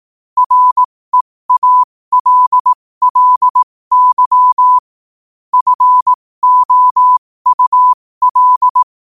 Morse code, heard as a keyed tone, about 18 wpm, spaced as for 15 wpm, 1000 Hz. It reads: REALLY FOUL
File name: qotdfiles / 15wpm